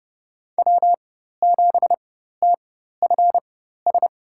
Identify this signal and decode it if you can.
Morse code, heard as a keyed tone, about 30 wpm, spaced as for 14 wpm, 700 Hz. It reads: W7TFS
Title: USA Call Signs - 5 Characters - 4x Character Spacing - 30wpm